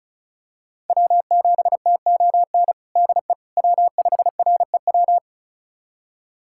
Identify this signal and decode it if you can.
Morse code, heard as a keyed tone, about 34 wpm, 700 Hz. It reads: W7TON DE W5REW
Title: USA Callsign DE Callsign - 5 Characters - 35wpm